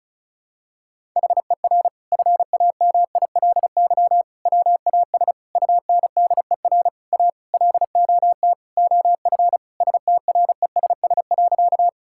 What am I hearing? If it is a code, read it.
Morse code, heard as a keyed tone, about 36 wpm, 700 Hz. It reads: HER FAMILY WAS UNDER A LOT OF STRESS.
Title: Sentences from Top 1000 Words - Review Part 6 - 35wpm